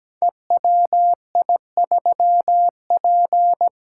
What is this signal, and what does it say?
Morse code, heard as a keyed tone, about 17 wpm, 700 Hz. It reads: EWI3P
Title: Code Groups - 5 Letter:Numbers - 17wpm